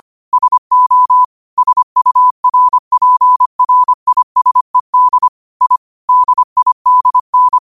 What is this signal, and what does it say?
Morse code, heard as a keyed tone, about 25 wpm, 1000 Hz. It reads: SO SURPRISED I DIDN